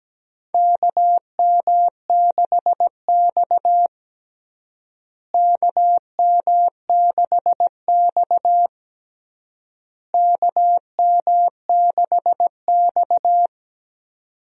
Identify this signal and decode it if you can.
Morse code, heard as a keyed tone, about 17 wpm, 700 Hz. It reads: KM6X KM6X KM6X
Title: USA Call Signs - 4 Characters - Repeated 3x - 17wpm